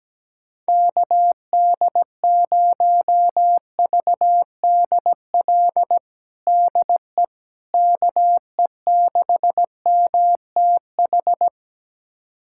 Morse code, heard as a keyed tone, about 17 wpm, 700 Hz. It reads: KD0VDL DE KE6MTH
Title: USA Callsign DE Callsign - 6 Characters - 17wpm